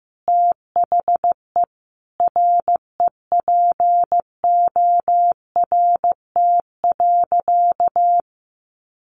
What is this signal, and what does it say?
Morse code, heard as a keyed tone, about 15 wpm, 700 Hz. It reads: THE REPORT.
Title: Sentences from Top 900 Words 15wpm